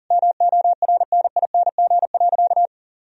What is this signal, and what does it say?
Morse code, heard as a keyed tone, about 40 wpm, 700 Hz. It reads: MORNING.
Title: Sentences from Top 1000 Words - Review Part 2 - 40wpm